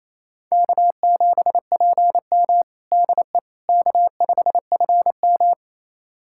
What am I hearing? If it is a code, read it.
Morse code, heard as a keyed tone, about 27 wpm, 700 Hz. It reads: K7PM DE K5FM
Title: USA Callsign DE Callsign - 4 Characters - 28wpm